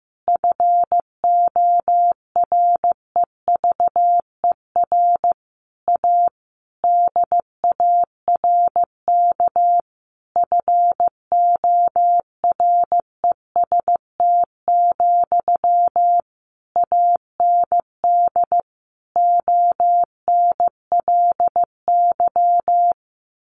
Morse code, heard as a keyed tone, about 15 wpm, 700 Hz. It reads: FOREVER A DARK FOREST, AND ONLY
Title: A Sound Of Thunder 15wpm